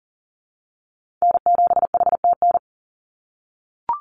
Morse code, heard as a keyed tone, about 40 wpm, 700 Hz. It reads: N7HTN
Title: USA Call Signs 40wpm